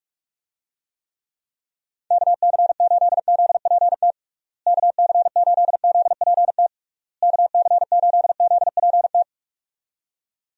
Morse code, heard as a keyed tone, about 46 wpm, 700 Hz. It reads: KC8ZPT KC8ZPT KC8ZPT
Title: USA Call Signs - 6 Characters - Repeated 3x - 45wpm